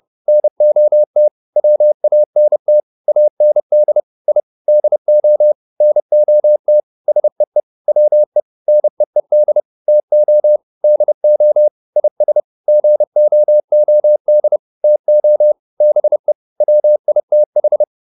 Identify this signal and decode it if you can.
Morse code, heard as a keyed tone, about 30 wpm, 600 Hz. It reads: NOT WANT AND I DO NOT SEE WE NEED TO DO IS GOOD TO BE WITH